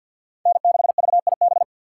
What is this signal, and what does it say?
Morse code, heard as a keyed tone, about 47 wpm, 700 Hz. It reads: N6VIB
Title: USA Call Signs - 4x Character Spacing - 50wpm